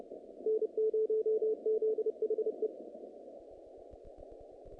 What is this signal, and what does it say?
Morse code, heard as a keyed tone, about 29 wpm, 440 Hz. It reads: N0ZHE
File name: contests / naqp-cw-k5tr-op-k5pi- -2200